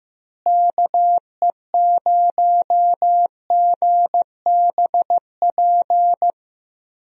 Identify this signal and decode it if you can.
Morse code, heard as a keyed tone, about 15 wpm, 700 Hz. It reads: KE0GBP
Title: USA Call Signs - 6 Characters - Repeated 3x - 15wpm